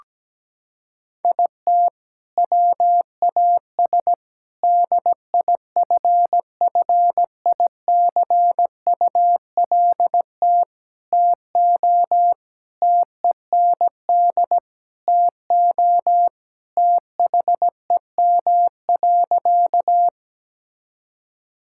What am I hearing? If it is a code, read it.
Morse code, heard as a keyed tone, about 17 wpm, 700 Hz. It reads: IT WAS DIFFICULT TO TEND TO THEM.
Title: Sentences from Top 1000 Words - Review Part 5 - 17wpm